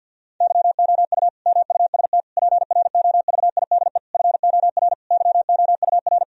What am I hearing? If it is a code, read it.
Morse code, heard as a keyed tone, about 50 wpm, 700 Hz. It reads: YOU MUST PROVIDE FOR YOUR
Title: Sentences from Top 1000 Words - Review Part 3 - 50wpm